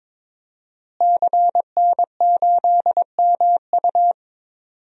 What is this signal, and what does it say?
Morse code, heard as a keyed tone, about 22 wpm, 700 Hz. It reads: CN8MU
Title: DXCC-100 Call Signs - 5 Characters - Repeated 3x - 22wpm